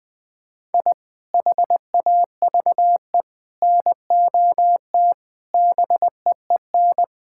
Morse code, heard as a keyed tone, about 20 wpm, 700 Hz. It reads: I HAVE NOT BEEN
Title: Sentences with 5-Word Phrases - Spoken Beginning - 20wpm